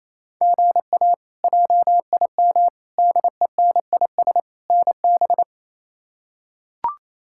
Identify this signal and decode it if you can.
Morse code, heard as a keyed tone, about 28 wpm, 700 Hz. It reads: GA JIM DENIS NB
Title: K1USN SST QSO Practice - 28wpm